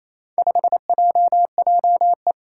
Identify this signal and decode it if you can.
Morse code, heard as a keyed tone, about 27 wpm, 700 Hz. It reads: 5JJE
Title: Four Challenging Letter-Number Combinations 28wpm